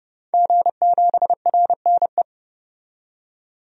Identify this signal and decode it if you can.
Morse code, heard as a keyed tone, about 31 wpm, 700 Hz. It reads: G7RNE
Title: DXCC-100 Call Signs - 5 Characters - 30wpm